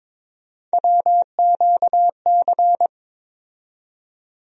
Morse code, heard as a keyed tone, about 21 wpm, 700 Hz. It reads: WQC
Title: USA Call Sign Postfixes - 3 Letters 22wpm